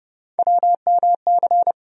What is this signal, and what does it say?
Morse code, heard as a keyed tone, about 30 wpm, 700 Hz. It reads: WMC
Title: USA Call Sign Postfixes - 3 Letters 30wpm